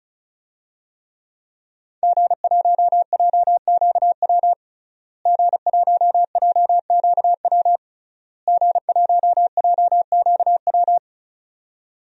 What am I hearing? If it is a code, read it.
Morse code, heard as a keyed tone, about 36 wpm, 700 Hz. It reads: G1JQW G1JQW G1JQW
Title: DXCC-100 Call Signs - 5 Characters - Repeated 3x - 35wpm